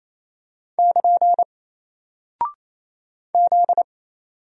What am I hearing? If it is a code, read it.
Morse code, heard as a keyed tone, about 28 wpm, 700 Hz. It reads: <KN> Z
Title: Single Character (Letter:Number:Punctuation:Prosign) Alt 28wpm